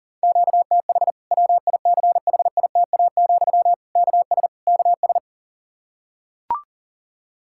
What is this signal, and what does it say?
Morse code, heard as a keyed tone, about 40 wpm, 700 Hz. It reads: QTH WICHITA, KS KS
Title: K1USN SST QSO Practice 40wpm